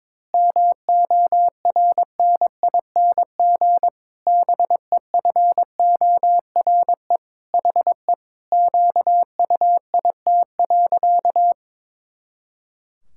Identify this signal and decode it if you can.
Morse code, heard as a keyed tone, about 22 wpm, 700 Hz. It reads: MORNING BEFORE HE QUIT.
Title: Pangram Challenge v2 - 22wpm